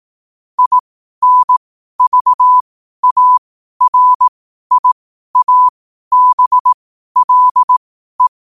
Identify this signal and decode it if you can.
Morse code, heard as a keyed tone, about 18 wpm, spaced as for 12 wpm, 1000 Hz. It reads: INVARIABLE
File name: qotdfiles / 12wpm